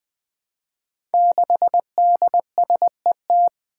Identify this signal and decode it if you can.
Morse code, heard as a keyed tone, about 21 wpm, 700 Hz. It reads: 6DSET
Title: Five Challenging Letter-Number Combinations 20wpm